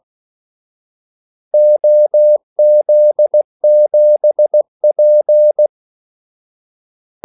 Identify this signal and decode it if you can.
Morse code, heard as a keyed tone, about 16 wpm, 600 Hz. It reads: OZ7P
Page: Kaldesignaler 16 wpm